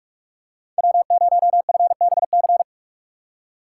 Morse code, heard as a keyed tone, about 46 wpm, 700 Hz. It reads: W0FBC
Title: USA Call Signs - 5 Characters - 6x Character Spacing - 45wpm